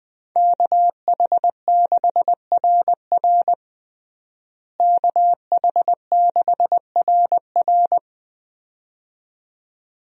Morse code, heard as a keyed tone, about 20 wpm, 700 Hz. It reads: KH6RR KH6RR
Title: USA Call Signs - 5 Characters - Repeated 2x - 20wpm